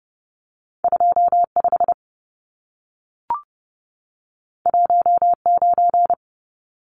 Morse code, heard as a keyed tone, about 30 wpm, 700 Hz. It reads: 25 19
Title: Numbers - 2-Digit 30wpm